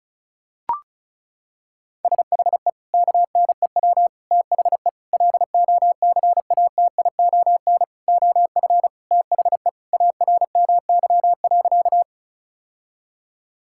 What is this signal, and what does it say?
Morse code, heard as a keyed tone, about 36 wpm, 690 Hz. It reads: SHE KNEW THE LOCATION OF THE ARMY.
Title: Sentences from Top 1000 Words - Review Part 6 - 35wpm